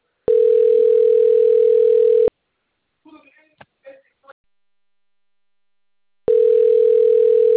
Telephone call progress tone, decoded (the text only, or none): ringback tone